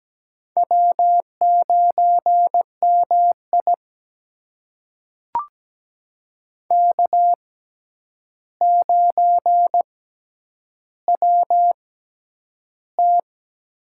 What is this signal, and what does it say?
Morse code, heard as a keyed tone, about 17 wpm, spaced as for 9 wpm, 700 Hz. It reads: W9MI K9WT
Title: USA Call Signs - 4 Characters - 6x Character Spacing - 17wpm